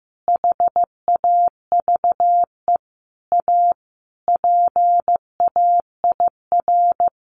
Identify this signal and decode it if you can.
Morse code, heard as a keyed tone, about 15 wpm, 700 Hz. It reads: HAVE A PAIR
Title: Sentences from Top 1200 Words 15wpm